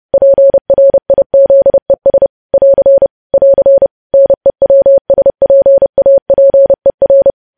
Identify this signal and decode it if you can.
Morse code, heard as a keyed tone, about 30 wpm, 550 Hz. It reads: PRIZES <AR> <AR> NEWSPAPER
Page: news headlines in morse code at 30 wpm